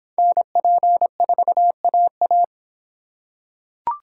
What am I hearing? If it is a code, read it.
Morse code, heard as a keyed tone, about 24 wpm, 700 Hz. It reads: NP4AA